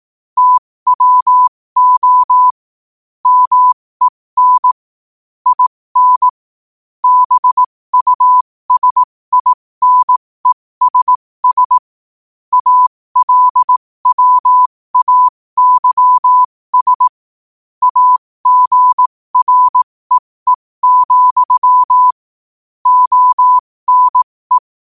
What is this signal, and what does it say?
Morse code, heard as a keyed tone, about 18 wpm, spaced as for 15 wpm, 1000 Hz. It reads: TWO MEN IN BUSINESS ALWAYS AGREE, ONE
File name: qotdfiles / 15wpm